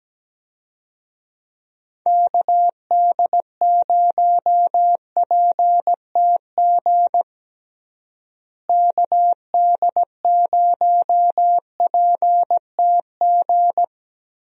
Morse code, heard as a keyed tone, about 17 wpm, 700 Hz. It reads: KD0PTG KD0PTG
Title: USA Call Signs - 6 Characters - Repeated 3x - 17wpm